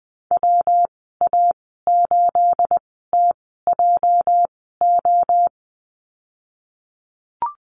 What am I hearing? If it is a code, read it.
Morse code, heard as a keyed tone, about 19 wpm, spaced as for 14 wpm, 700 Hz. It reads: WA8TJO
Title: 213 Lesson 34g - Number 8 - Call Signs 20wpm